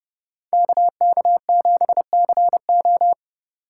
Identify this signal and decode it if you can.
Morse code, heard as a keyed tone, about 37 wpm, 700 Hz. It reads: KK7CO
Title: Speed Racing - USA 2x2 Call Signs 30wpm